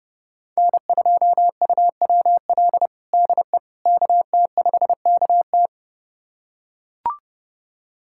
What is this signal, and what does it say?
Morse code, heard as a keyed tone, about 30 wpm, 700 Hz. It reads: N2UWL DE KT5KT
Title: USA Callsign DE Callsign - 5 Characters - 30wpm